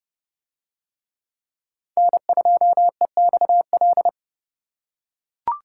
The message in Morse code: N2EXL

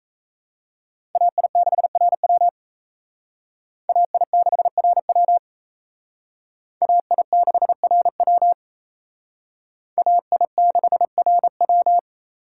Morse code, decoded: AI6RW AI6RW AI6RW AI6RW

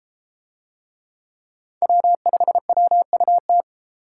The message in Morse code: W5WUT

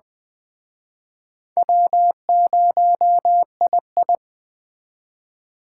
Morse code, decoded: W0II

